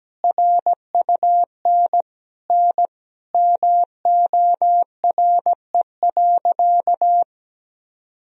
Morse code, heard as a keyed tone, about 17 wpm, 700 Hz. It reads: RUN N MORE.
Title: Sentences from Top 300 Words and Abbreviations 17wpm